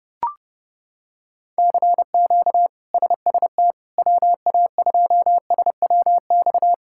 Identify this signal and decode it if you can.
Morse code, heard as a keyed tone, about 32 wpm, 700 Hz. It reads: CQ SST WA2SWX
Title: K1USN SST QSO Practice - 30wpm